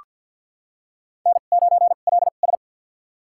Morse code, heard as a keyed tone, about 47 wpm, spaced as for 35 wpm, 700 Hz. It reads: N9LS
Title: USA Call Signs - 4 Characters - 2x Character Spacing - 50wpm